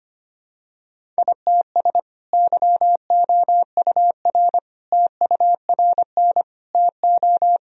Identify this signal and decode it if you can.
Morse code, heard as a keyed tone, about 25 wpm, 700 Hz. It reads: ITS YOUR TURN TO
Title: Sentences from Top 1000 Words - Review Part 1 - 25wpm